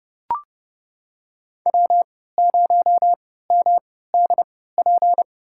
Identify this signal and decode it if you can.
Morse code, heard as a keyed tone, about 30 wpm, spaced as for 17 wpm, 700 Hz. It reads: W0MDP
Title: USA Call Signs - 5 Characters - 3x Character Spacing - 30wpm